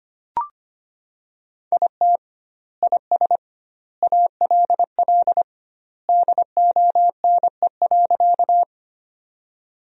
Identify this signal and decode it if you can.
Morse code, heard as a keyed tone, about 25 wpm, 700 Hz. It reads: IT IS ALL DONE.
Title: 2x Word-Spacing - Sentences from Top 500 Words - 25wpm